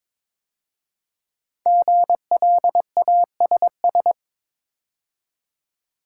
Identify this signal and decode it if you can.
Morse code, heard as a keyed tone, about 23 wpm, 700 Hz. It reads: GLASS
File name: 5 Letter Words - First 3 Letters Given - 22wpm